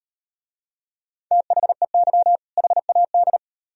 Morse code, heard as a keyed tone, about 36 wpm, 700 Hz. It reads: THEY HAD